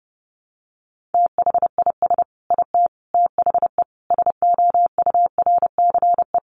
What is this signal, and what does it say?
Morse code, heard as a keyed tone, about 30 wpm, 700 Hz. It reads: THIS IT THE SOURCE
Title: Sentences from Top 800 Words 30wpm